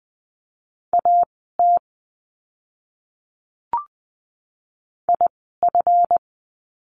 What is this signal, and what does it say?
Morse code, heard as a keyed tone, about 20 wpm, spaced as for 14 wpm, 700 Hz. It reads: AT IF
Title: 185 Lesson 30f - Letter b - Words 20wpm